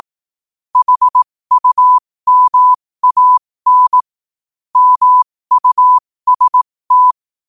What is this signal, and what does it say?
Morse code, heard as a keyed tone, about 18 wpm, spaced as for 15 wpm, 1000 Hz. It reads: HUMAN MUST
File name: qotdfiles / 15wpm